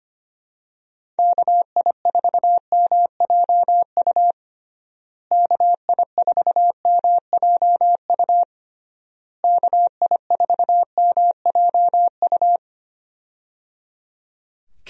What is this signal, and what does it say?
Morse code, heard as a keyed tone, about 25 wpm, 700 Hz. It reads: KI4MJU KI4MJU KI4MJU
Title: USA Call Signs - 6 Characters - Repeated 3x - 25wpm